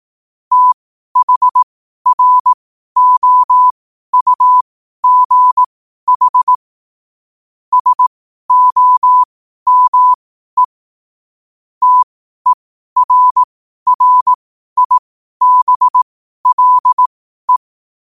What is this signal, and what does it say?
Morse code, heard as a keyed tone, about 18 wpm, spaced as for 12 wpm, 1000 Hz. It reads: THROUGH SOME TERRIBLE
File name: qotdfiles / 12wpm